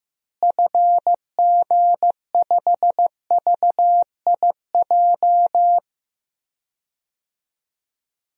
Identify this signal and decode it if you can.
Morse code, heard as a keyed tone, about 15 wpm, 700 Hz. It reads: FG5VIJ
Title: DXCC-100 Call Signs - 6 Characters - Repeated 2x - 15wpm